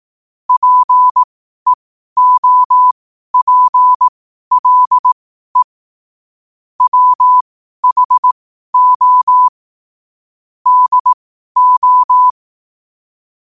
Morse code, heard as a keyed tone, about 18 wpm, spaced as for 12 wpm, 1000 Hz. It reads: PEOPLE WHO DO
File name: qotdfiles / 12wpm